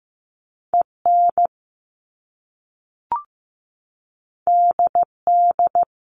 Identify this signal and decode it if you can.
Morse code, heard as a keyed tone, about 15 wpm, 700 Hz. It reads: EN DD
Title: Top Two Letter Combinations in Words 15wpm